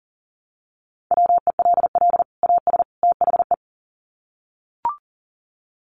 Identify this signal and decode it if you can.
Morse code, heard as a keyed tone, about 40 wpm, 700 Hz. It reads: WELL AS THE
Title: Sets of 3 Words 40wpm